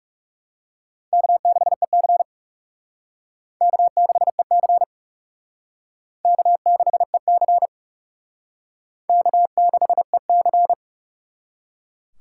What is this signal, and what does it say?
Morse code, heard as a keyed tone, about 36 wpm, 700 Hz. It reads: K6EC K6EC K6EC K6EC
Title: Speed Racing - USA Call Signs 30wpm